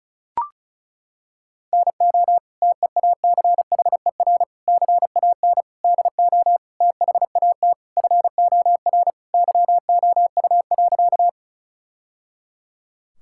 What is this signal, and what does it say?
Morse code, heard as a keyed tone, about 36 wpm, 700 Hz. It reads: NO TEACHER CAN DO THAT FOR YOU.
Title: Sentences from Top 1000 Words - Review Part 6 - 35wpm